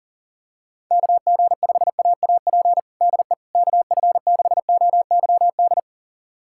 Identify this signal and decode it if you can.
Morse code, heard as a keyed tone, about 40 wpm, 700 Hz. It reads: KG5AAP DE KF6OYD
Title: USA Callsign DE Callsign - 6 Characters - 40wpm